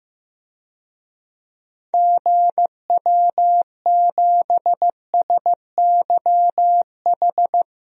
Morse code, heard as a keyed tone, about 15 wpm, 700 Hz. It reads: GW7SYH
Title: DXCC-100 Call Signs - 6 Characters - 15wpm